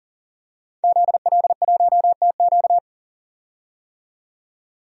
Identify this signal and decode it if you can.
Morse code, heard as a keyed tone, about 40 wpm, 700 Hz. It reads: ZL1TQ
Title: DXCC-100 Call Signs - 5 Characters - 40wpm